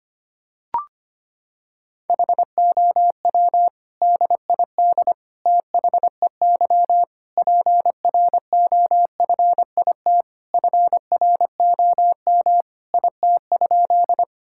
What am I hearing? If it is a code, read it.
Morse code, heard as a keyed tone, about 25 wpm, 700 Hz. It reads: HOW DID THEY PROFIT FROM IT?
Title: Sentences from Top 1000 Words - Review Part 5 - 25wpm